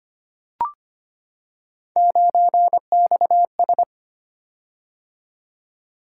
Morse code, heard as a keyed tone, about 25 wpm, 700 Hz. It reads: 9XS
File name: Three Letter-Number Combinations - 25wpm